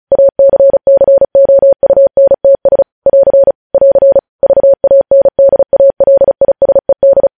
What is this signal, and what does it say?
Morse code, heard as a keyed tone, about 36 wpm, 550 Hz. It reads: ACCOUNTS <AR> <AR> VANDALISED
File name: podcast / News headlines-1x-35x35